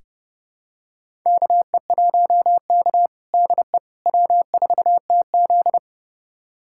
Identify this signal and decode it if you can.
Morse code, heard as a keyed tone, about 30 wpm, 700 Hz. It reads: KE1K DE W4TZ
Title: USA Callsign DE Callsign - 4 Characters - 30wpm